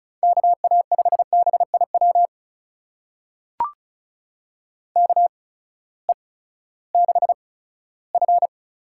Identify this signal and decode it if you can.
Morse code, heard as a keyed tone, about 34 wpm, spaced as for 17 wpm, 700 Hz. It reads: KA5BIW KE6F
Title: USA Call Signs - 6 Characters - 8x Character Spacing - 35wpm